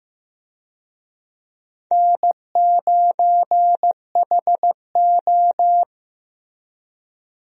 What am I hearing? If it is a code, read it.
Morse code, heard as a keyed tone, about 15 wpm, 700 Hz. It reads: N9HO
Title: USA Call Signs - 4 Characters - Repeated 2x - 15wpm